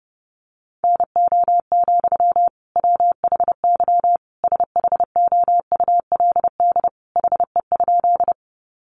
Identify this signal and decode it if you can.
Morse code, heard as a keyed tone, about 30 wpm, 700 Hz. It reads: NO, WHY SHOULD HE?
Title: Sentences from Top 300 Words 30wpm